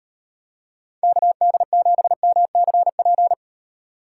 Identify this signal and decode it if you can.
Morse code, heard as a keyed tone, about 38 wpm, 700 Hz. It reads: KD7MCP